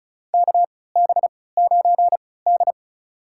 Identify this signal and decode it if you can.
Morse code, heard as a keyed tone, about 34 wpm, spaced as for 20 wpm, 700 Hz. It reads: KB9D
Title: USA Call Signs - 4 Characters - 3x Character Spacing - 35wpm